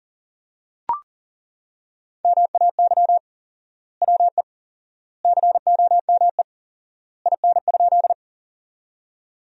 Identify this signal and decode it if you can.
Morse code, heard as a keyed tone, about 40 wpm, 700 Hz. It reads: MAY WE COME IN?
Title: 4x Word-Spacing - Sentences from Top 100 Words - 40wpm